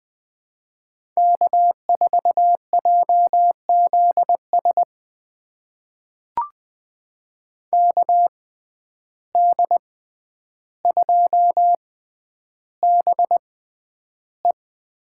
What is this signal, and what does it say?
Morse code, heard as a keyed tone, about 20 wpm, spaced as for 10 wpm, 700 Hz. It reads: K4JZS KD2BE
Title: USA Call Signs - 5 Characters - 6x Character Spacing - 20wpm